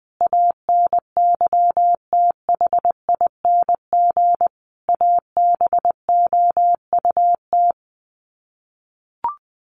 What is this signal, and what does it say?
Morse code, heard as a keyed tone, about 20 wpm, 700 Hz. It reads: ANYTHING ABOUT
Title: Sets of 5 Words Encore2 20wpm